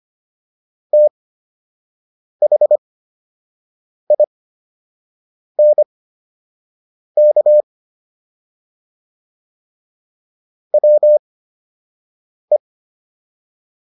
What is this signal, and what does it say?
Morse code, heard as a keyed tone, about 25 wpm, spaced as for 6 wpm, 600 Hz. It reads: THINK WE